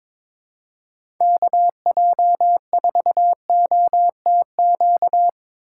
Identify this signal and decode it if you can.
Morse code, heard as a keyed tone, about 22 wpm, 700 Hz. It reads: KJ4OTQ
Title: USA Call Signs - 6 Characters - 4x Character Spacing - 22wpm